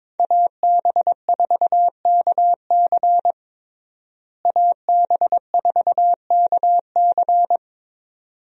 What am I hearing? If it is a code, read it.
Morse code, heard as a keyed tone, about 22 wpm, 700 Hz. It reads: AB4KC AB4KC
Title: USA Call Signs - 5 Characters - Repeated 2x - 22wpm